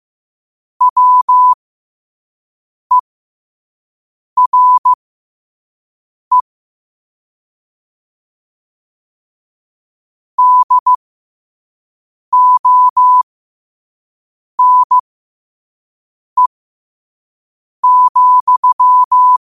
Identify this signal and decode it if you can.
Morse code, heard as a keyed tone, about 15 wpm, spaced as for 5 wpm, 1000 Hz. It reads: WERE DONE,
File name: qotdfiles / 5wpm